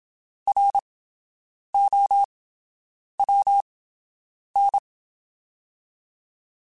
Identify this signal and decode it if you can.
Morse code, heard as a keyed tone, about 27 wpm, 800 Hz. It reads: R O W N
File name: TH / TH mp3s / Fast TH @25